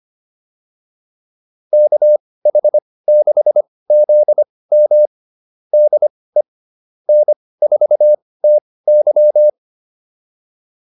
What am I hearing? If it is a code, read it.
Morse code, heard as a keyed tone, about 25 wpm, spaced as for 18 wpm, 600 Hz. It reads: KH6ZM DE N4TY